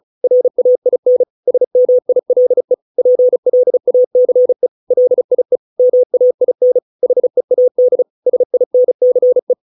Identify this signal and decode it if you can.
Morse code, heard as a keyed tone, about 34 wpm, 500 Hz. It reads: RAIN SMILE PLACE LIE MAIN HEAD SINCE